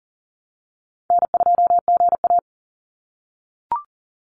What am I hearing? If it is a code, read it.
Morse code, heard as a keyed tone, about 40 wpm, 700 Hz. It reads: N2GA
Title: Call Signs - 40wpm